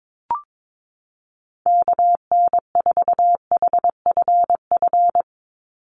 Morse code, heard as a keyed tone, about 22 wpm, 700 Hz. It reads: KN4HFF
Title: USA Call Signs 22wpm